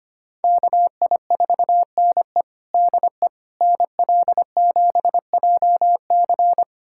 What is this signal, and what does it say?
Morse code, heard as a keyed tone, about 25 wpm, 700 Hz. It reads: KI4NE DE NL7JC
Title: USA Callsign DE Callsign 25wpm